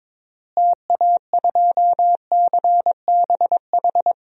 Morse code, heard as a keyed tone, about 22 wpm, 700 Hz. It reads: TA2CBH
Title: DXCC-100 Call Signs - 6 Characters - Repeated 2x - 22wpm